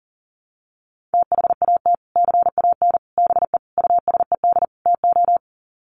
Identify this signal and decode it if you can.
Morse code, heard as a keyed tone, about 37 wpm, 700 Hz. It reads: THAT CAN BE USED TO
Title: Sets of 5 Words Encore 40wpm